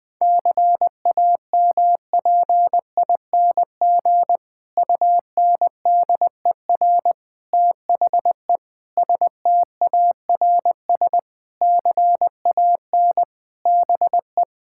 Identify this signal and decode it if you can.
Morse code, heard as a keyed tone, about 20 wpm, 700 Hz. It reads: CAMPING UNDER THE STARS CAN BE
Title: Difficult Sentences with Hints - 20wpm